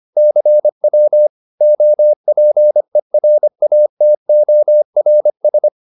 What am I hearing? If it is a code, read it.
Morse code, heard as a keyed tone, about 25 wpm, 600 Hz. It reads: CW OPERATORS